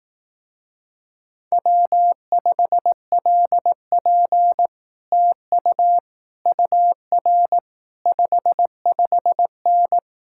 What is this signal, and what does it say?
Morse code, heard as a keyed tone, about 18 wpm, 700 Hz. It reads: W5LP TU UR 55N
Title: Speed-Racing Common QSO Phrases Encore - 15wpm